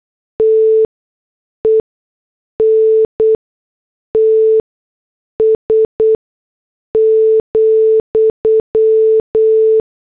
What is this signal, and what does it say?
Morse code, heard as a keyed tone, about 8 wpm, spaced as for 6 wpm, 440 Hz. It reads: TENTS,
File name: morse / headlines